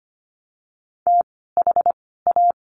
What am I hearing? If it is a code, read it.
Morse code, heard as a keyed tone, about 25 wpm, 700 Hz. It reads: T H A
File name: course / 25:10wpm / 094 Lesson 16f - Letter c - Words 25wpm